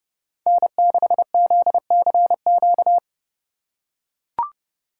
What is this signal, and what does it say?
Morse code, heard as a keyed tone, about 30 wpm, 700 Hz. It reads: N6ZCQ